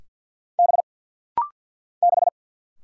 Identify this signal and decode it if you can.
Morse code, heard as a keyed tone, about 50 wpm, 690 Hz. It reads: B 6
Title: Single Letter-Number Alt - Rapid-Fire 50wpm